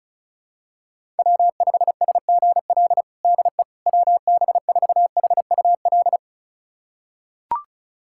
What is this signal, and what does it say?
Morse code, heard as a keyed tone, about 36 wpm, 700 Hz. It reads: W5SGL DE WB4HUL
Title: USA Callsign DE Callsign 35wpm